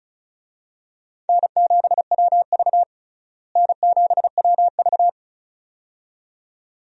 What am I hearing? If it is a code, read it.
Morse code, heard as a keyed tone, about 34 wpm, 700 Hz. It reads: N7WV N7WV
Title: USA Call Signs - 4 Characters - Repeated 2x - 35wpm